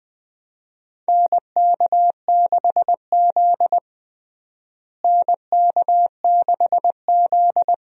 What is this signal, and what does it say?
Morse code, heard as a keyed tone, about 19 wpm, 700 Hz. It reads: NK6Z NK6Z
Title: USA Call Signs - 4 Characters - Repeated 3x - 20wpm